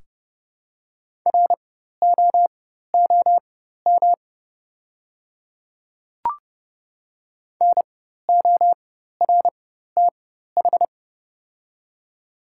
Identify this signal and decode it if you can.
Morse code, heard as a keyed tone, about 30 wpm, spaced as for 14 wpm, 700 Hz. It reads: ROOM NORTH
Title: 4x Character Spacing - Top 300 Words - 30wpm